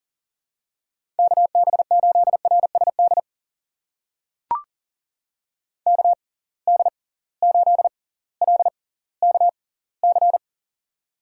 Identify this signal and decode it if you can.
Morse code, heard as a keyed tone, about 39 wpm, spaced as for 20 wpm, 700 Hz. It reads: KB8RSD KD8LKC
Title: USA Call Signs - 6 Characters - 6x Character Spacing - 40wpm